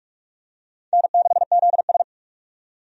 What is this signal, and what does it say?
Morse code, heard as a keyed tone, about 47 wpm, 700 Hz. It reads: N6ZS
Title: USA Call Signs - 4 Characters - 45wpm